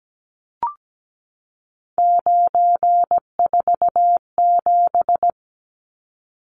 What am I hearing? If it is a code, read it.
Morse code, heard as a keyed tone, about 17 wpm, 700 Hz. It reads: 947
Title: Numbers - 3-Digit 17wpm